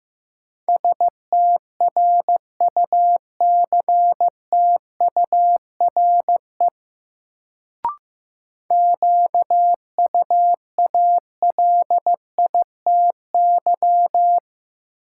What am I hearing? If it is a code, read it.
Morse code, heard as a keyed tone, about 16 wpm, 700 Hz. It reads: STRUCTURE QUALITY
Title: Sentences from Top 1500 Words - New Word Drill Alt - 15wpm